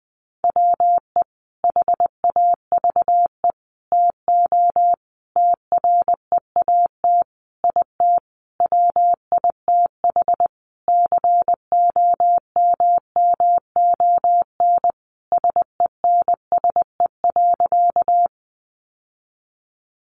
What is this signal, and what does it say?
Morse code, heard as a keyed tone, about 20 wpm, 700 Hz. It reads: WE HAVE TO TREAT IT WITH COMMON SENSE.
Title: Sentences from Top 1300 Words 20wpm